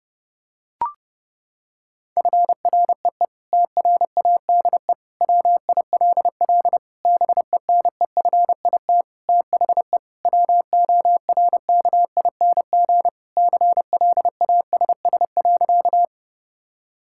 Morse code, heard as a keyed tone, about 30 wpm, 700 Hz. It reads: FREE TRADE WILL BENEFIT THE WORKING CLASS.
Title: Sentences from Top 1000 Words - Review Part 6 - 30wpm